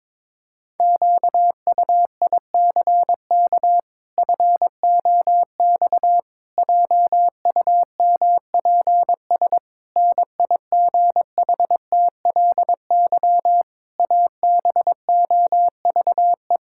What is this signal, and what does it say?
Morse code, heard as a keyed tone, about 22 wpm, 700 Hz. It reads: QUICK FOX JUMPS NIGHTLY ABOVE
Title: Pangram Challenge v2 - 22wpm